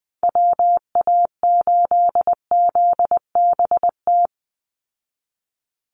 Morse code, heard as a keyed tone, about 20 wpm, 700 Hz. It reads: WA8ZBT
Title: Call Signs Repeated 3x - 20wpm